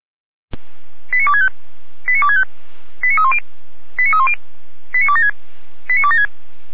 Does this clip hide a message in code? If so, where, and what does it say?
Yes: five-tone selective call (ZVEI) from 1.1 to 6.3 s, heard as quick runs of five tones, tones of about 70 ms. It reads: 89265 89265 89310 89310 89276 89276